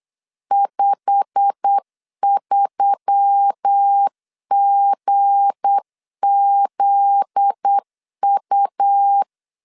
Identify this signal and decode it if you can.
Morse code, heard as a keyed tone, about 8 wpm, 800 Hz. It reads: 53GZU
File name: LEZIONI / 8 YJ45 / 8 YJ45 8wpm 3214